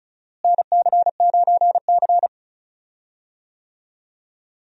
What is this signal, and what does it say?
Morse code, heard as a keyed tone, about 36 wpm, 700 Hz. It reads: NC9C
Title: USA Call Signs - 4 Characters - Repeated 3x - 35wpm